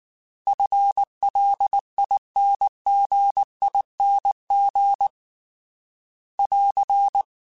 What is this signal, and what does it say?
Morse code, heard as a keyed tone, about 19 wpm, 800 Hz. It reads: FLINGING <AR>
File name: FL / FL mp3s / FL Prefixes 3x2 Twice @18